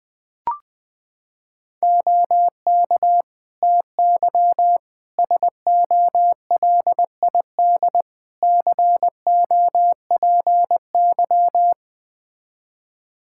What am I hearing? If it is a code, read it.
Morse code, heard as a keyed tone, about 20 wpm, 700 Hz. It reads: OK TY SOLID COPY
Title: K1USN SST QSO Practice 20wpm